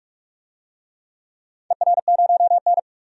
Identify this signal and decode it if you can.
Morse code, heard as a keyed tone, about 44 wpm, 700 Hz. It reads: ER0N
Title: DXCC-100 Call Signs - 4 Characters - 45wpm